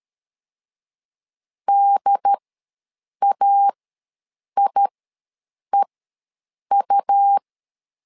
Morse code, heard as a keyed tone, about 13 wpm, spaced as for 7 wpm, 800 Hz. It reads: DAIEU
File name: LEZIONI / 4 Q ws 12wpm 1912